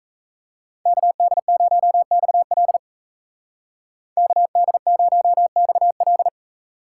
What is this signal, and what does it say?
Morse code, heard as a keyed tone, about 34 wpm, 700 Hz. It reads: KD0XL KD0XL
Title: Speed Racing - USA 2x2 Call Signs 28wpm